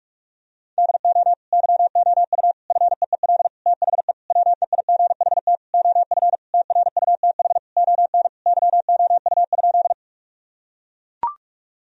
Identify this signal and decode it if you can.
Morse code, heard as a keyed tone, about 43 wpm, 700 Hz. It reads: DO YOU FEEL THE WEIGHT OF TRUTH ON YOU?
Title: Sentences from Top 1000 Words - Review Part 3 - 45wpm